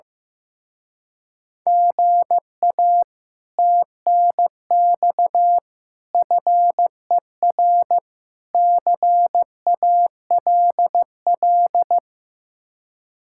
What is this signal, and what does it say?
Morse code, heard as a keyed tone, about 15 wpm, 700 Hz. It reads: GA TNX FER CALL
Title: Speed-Racing Common QSO Phrases Encore - 15wpm